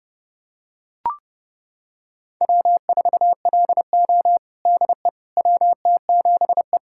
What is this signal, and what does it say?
Morse code, heard as a keyed tone, about 30 wpm, 700 Hz. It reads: W4LO DE WT7E
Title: USA Callsign DE Callsign - 4 Characters - 30wpm